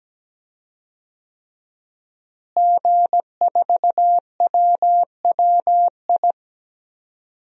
Morse code, heard as a keyed tone, about 17 wpm, 700 Hz. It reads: G4WWI
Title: DXCC-100 Call Signs - 5 Characters - Repeated 3x - 17wpm